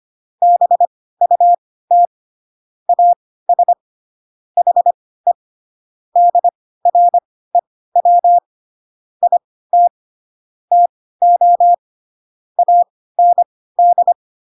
Morse code, heard as a keyed tone, about 25 wpm, spaced as for 16 wpm, 700 Hz. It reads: BUT AS HE DREW IT TO AND
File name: Pinocchio / Pinocchio ch1 10 25 ik1hge